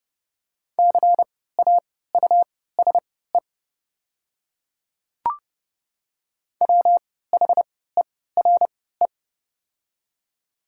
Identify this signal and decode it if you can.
Morse code, heard as a keyed tone, about 29 wpm, spaced as for 17 wpm, 690 Hz. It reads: CAUSE WHERE
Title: 3x Character Spacing - Top 300 Words - 30wpm